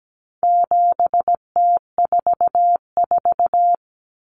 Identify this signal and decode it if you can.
Morse code, heard as a keyed tone, about 17 wpm, 700 Hz. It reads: 7T44
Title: Common Cut Numbers - 4-Digit 17wpm